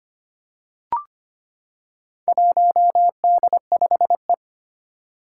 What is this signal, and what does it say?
Morse code, heard as a keyed tone, about 25 wpm, 700 Hz. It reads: 1D5E
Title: Four Challenging Letter-Number Combinations 25wpm